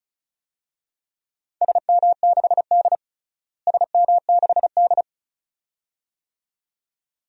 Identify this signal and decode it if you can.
Morse code, heard as a keyed tone, about 34 wpm, 700 Hz. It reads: SM6D SM6D
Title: DXCC-100 Call Signs - 4 Characters - Repeated 2x - 35wpm